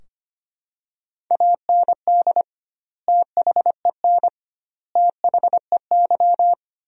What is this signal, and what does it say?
Morse code, heard as a keyed tone, about 25 wpm, 700 Hz. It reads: AND THEN THEY
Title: 2x Word-Spacing - Sentences from Top 300 Words - 25wpm